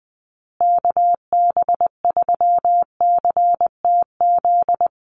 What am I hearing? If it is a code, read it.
Morse code, heard as a keyed tone, about 20 wpm, 700 Hz. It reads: KB3CTZ
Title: USA Call Signs Repeated 3x 20wpm